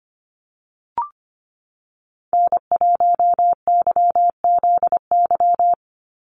Morse code, heard as a keyed tone, about 25 wpm, 700 Hz. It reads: N1YZY
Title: USA Call Signs 25wpm